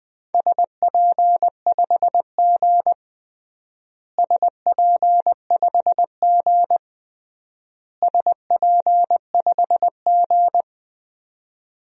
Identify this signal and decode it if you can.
Morse code, heard as a keyed tone, about 21 wpm, 700 Hz. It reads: SP5G SP5G SP5G
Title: DXCC-100 Call Signs - 4 Characters - Repeated 3x - 20wpm